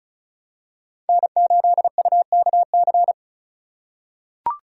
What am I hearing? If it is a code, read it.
Morse code, heard as a keyed tone, about 36 wpm, 700 Hz. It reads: N8UKC
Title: USA Call Signs - 5 Characters - 3x Character Spacing - 35wpm